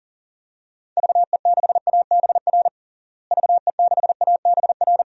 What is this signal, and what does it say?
Morse code, heard as a keyed tone, about 40 wpm, 700 Hz. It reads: VE6ABR VE6ABR
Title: DXCC-100 Call Signs - 6 Characters - Repeated 2x - 40wpm